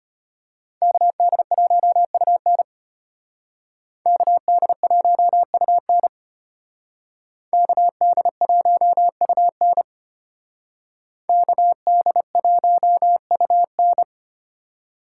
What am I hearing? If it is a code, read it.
Morse code, heard as a keyed tone, about 32 wpm, 700 Hz. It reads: KD1UN KD1UN KD1UN KD1UN